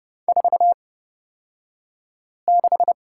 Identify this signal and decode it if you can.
Morse code, heard as a keyed tone, about 30 wpm, spaced as for 5 wpm, 700 Hz. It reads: 46
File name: Lesson 38d - Letter q - Two Characters 30wpm